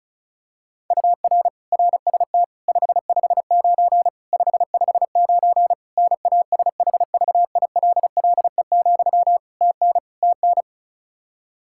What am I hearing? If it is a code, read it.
Morse code, heard as a keyed tone, about 36 wpm, 700 Hz. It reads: UR RST 559 559 NASHVILLE, TN TN
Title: Practice QSO - Exchange 1 - Beginning - 35wpm